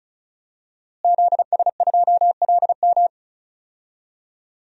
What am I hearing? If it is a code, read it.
Morse code, heard as a keyed tone, about 36 wpm, 700 Hz. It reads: ZS2LM
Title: DXCC-100 Call Signs - 5 Characters - Repeated 3x - 35wpm